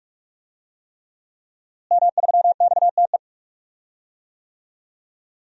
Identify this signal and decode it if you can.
Morse code, heard as a keyed tone, about 42 wpm, 700 Hz. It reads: M3XTE
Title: DXCC-100 Call Signs - 5 Characters - Repeated 3x - 45wpm